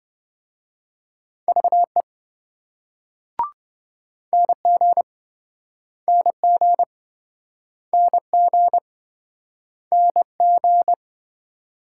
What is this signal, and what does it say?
Morse code, heard as a keyed tone, about 29 wpm, 700 Hz. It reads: VE NG NG NG NG